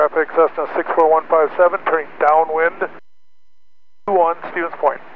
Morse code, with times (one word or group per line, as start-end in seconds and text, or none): none